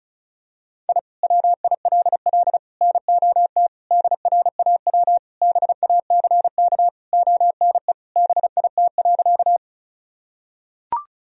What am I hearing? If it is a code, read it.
Morse code, heard as a keyed tone, about 36 wpm, 700 Hz. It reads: I WILL NOT DRAW BACK ONE BIT.
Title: Sentences from Top 1000 Words - Review Part 1 - 35wpm